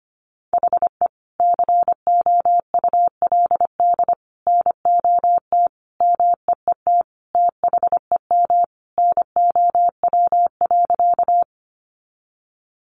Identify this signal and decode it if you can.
Morse code, heard as a keyed tone, about 25 wpm, 700 Hz. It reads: HE COULD NOT MEET THEM NOW.
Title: Sentences from Top 1100 Words 25wpm